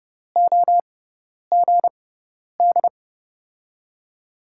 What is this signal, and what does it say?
Morse code, heard as a keyed tone, about 30 wpm, spaced as for 10 wpm, 700 Hz. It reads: OGD
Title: USA Call Signs - 5 Characters - 6x Character Spacing - 30wpm